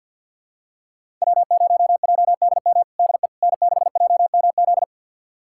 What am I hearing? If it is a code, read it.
Morse code, heard as a keyed tone, about 46 wpm, 700 Hz. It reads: W0JDM DE N6JMZ
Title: USA Callsign DE Callsign 50wpm